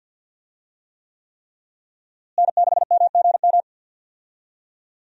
Morse code, heard as a keyed tone, about 52 wpm, 700 Hz. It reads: N6MGM
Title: USA Call Signs - 5 Characters - Repeated 3x - 50wpm